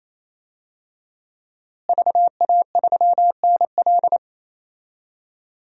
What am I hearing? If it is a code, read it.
Morse code, heard as a keyed tone, about 27 wpm, 700 Hz. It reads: VA3NL
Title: DXCC-100 Call Signs - 5 Characters - Repeated 3x - 28wpm